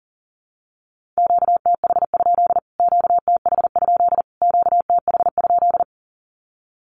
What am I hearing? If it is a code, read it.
Morse code, heard as a keyed tone, about 40 wpm, 700 Hz. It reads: QTH? QTH? QTH?
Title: Q-Codes Repeated 3x 40wpm